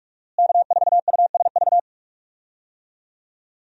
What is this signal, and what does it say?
Morse code, heard as a keyed tone, about 43 wpm, 700 Hz. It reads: K4USV
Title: USA Call Signs - 5 Characters - Repeated 2x - 45wpm